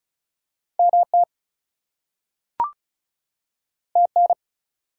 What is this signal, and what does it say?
Morse code, heard as a keyed tone, about 36 wpm, 700 Hz. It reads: MT TN
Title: US States and Canadian Provinces 35wpm